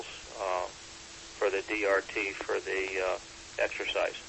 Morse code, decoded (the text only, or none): none